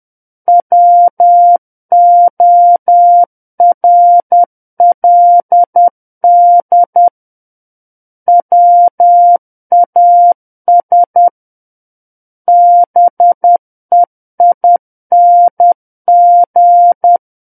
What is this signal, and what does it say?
Morse code, heard as a keyed tone, about 10 wpm, 700 Hz. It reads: WORLD WAS BEING
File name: Book One / 01-Chapter 1 - 10 WPM